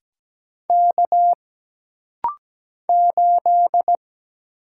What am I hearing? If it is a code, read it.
Morse code, heard as a keyed tone, about 17 wpm, 700 Hz. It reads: K 8
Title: Single Letter-Number - Rapid-Fire 17wpm